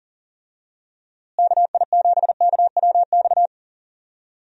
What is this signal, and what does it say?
Morse code, heard as a keyed tone, about 40 wpm, 700 Hz. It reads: KI7KWX
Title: USA Call Signs - 6 Characters - Repeated 3x - 40wpm